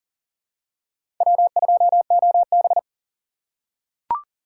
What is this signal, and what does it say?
Morse code, heard as a keyed tone, about 40 wpm, 700 Hz. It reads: W2OB
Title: USA Call Signs - 4 Characters - 2x Character Spacing - 40wpm